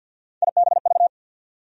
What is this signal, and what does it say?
Morse code, heard as a keyed tone, about 53 wpm, 700 Hz. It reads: IBV
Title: USA Call Sign Postfixes - 3 Letters 50wpm